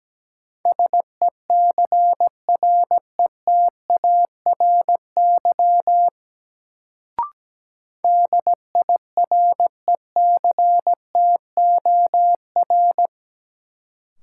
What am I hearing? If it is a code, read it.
Morse code, heard as a keyed tone, about 17 wpm, 700 Hz. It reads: SECRETARY DIRECTOR
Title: Sentences from Top 1300 Words - New Word Drill Alt - 17wpm